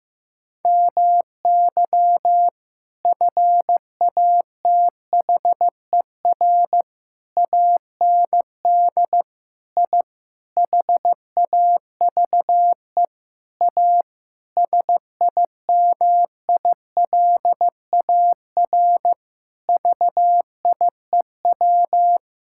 Morse code, heard as a keyed tone, about 15 wpm, 700 Hz. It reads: MY FATHER AND I HAVE A SIMILAR VIEW
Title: Sentences from Top 1000 Words - Review Part 6 - 15wpm